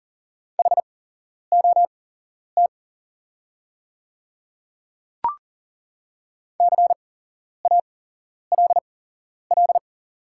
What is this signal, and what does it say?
Morse code, heard as a keyed tone, about 37 wpm, spaced as for 11 wpm, 700 Hz. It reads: HOT CALL
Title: 8x Character Spacing - Top 200 Words - 40wpm